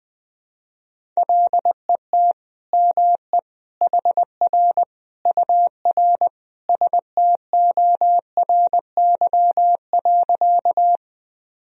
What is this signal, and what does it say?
Morse code, heard as a keyed tone, about 21 wpm, 700 Hz. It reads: LET ME HR UR STORY.
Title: Sentences from Top 300 Words and Abbreviations 20wpm